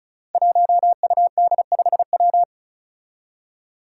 Morse code, heard as a keyed tone, about 36 wpm, 700 Hz. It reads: 1UD5W
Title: Five Challenging Letter-Number Combinations 35wpm